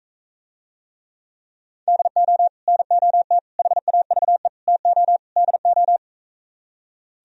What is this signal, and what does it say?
Morse code, heard as a keyed tone, about 43 wpm, 700 Hz. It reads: DO NOT HAVE TO DO